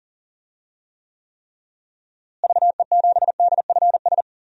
Morse code, heard as a keyed tone, about 40 wpm, 700 Hz. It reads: VE7DFS
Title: DXCC-100 Call Signs - 6 Characters - Repeated 3x - 40wpm